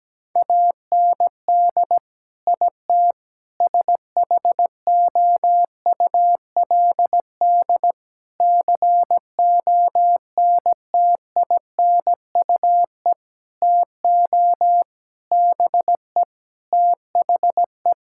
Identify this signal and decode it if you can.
Morse code, heard as a keyed tone, about 17 wpm, 700 Hz. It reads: AND IT SHOULD CONTINUE TO BE THE
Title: Sentences from Top 1000 Words - Review Part 6 - 17wpm